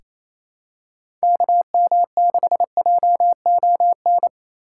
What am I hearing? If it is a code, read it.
Morse code, heard as a keyed tone, about 28 wpm, 700 Hz. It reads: KM6JON
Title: USA Call Signs - 6 Characters - 3x Character Spacing - 28wpm